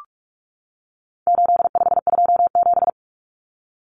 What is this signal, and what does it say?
Morse code, heard as a keyed tone, about 44 wpm, 700 Hz. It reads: 8527
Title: Common Cut Numbers - 4-Digit 45wpm